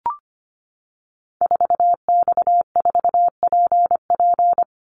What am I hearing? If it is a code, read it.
Morse code, heard as a keyed tone, about 25 wpm, 700 Hz. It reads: 4X4PP
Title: Call Signs - 25wpm